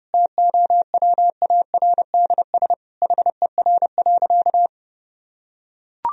Morse code, heard as a keyed tone, about 30 wpm, 700 Hz. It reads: TOWARDS HER.
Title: Sentences from Top 1000 Words - Review Part 2 - 30wpm